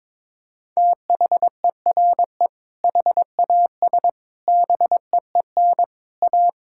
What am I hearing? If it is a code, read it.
Morse code, heard as a keyed tone, about 22 wpm, 700 Hz. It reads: THERE HAS BEEN A
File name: practice / 22wpm / Sentences with 8-Word Phrases - 22wpm